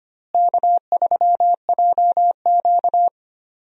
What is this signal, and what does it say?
Morse code, heard as a keyed tone, about 25 wpm, 700 Hz. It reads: K3JQ
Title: USA Call Signs - 4 Characters - Repeated 3x - 25wpm